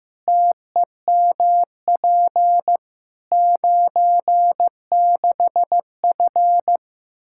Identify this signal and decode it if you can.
Morse code, heard as a keyed tone, about 15 wpm, 700 Hz. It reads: TEMP 96F
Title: Practice QSO - Exchange 2 - Info - 15wpm